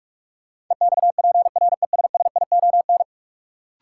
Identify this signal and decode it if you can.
Morse code, heard as a keyed tone, about 46 wpm, 700 Hz. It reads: EXPRESSION
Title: Sentences from Top 1200 Words - New Word Drill Alt - 45wpm